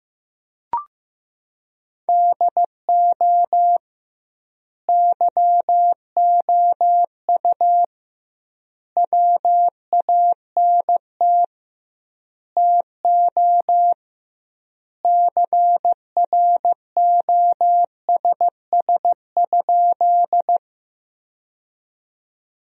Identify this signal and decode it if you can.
Morse code, heard as a keyed tone, about 15 wpm, 700 Hz. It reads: DO YOU WANT TO CROSS?
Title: 2x Word-Spacing - Sentences from Top 300 Words - 15wpm